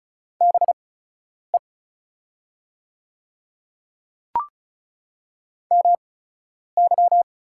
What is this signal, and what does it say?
Morse code, heard as a keyed tone, about 36 wpm, spaced as for 10 wpm, 700 Hz. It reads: BE MY
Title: 8x Character Spacing - Top 300 Words - 35wpm